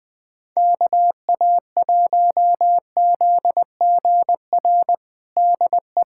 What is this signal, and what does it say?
Morse code, heard as a keyed tone, about 20 wpm, 700 Hz. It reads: KA1ZGR DE
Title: USA Callsign DE Callsign - 6 Characters - 20wpm